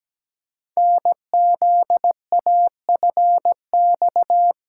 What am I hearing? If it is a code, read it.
Morse code, heard as a keyed tone, about 17 wpm, 700 Hz. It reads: NZAFX
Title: Code Groups - 5 Letters - 17wpm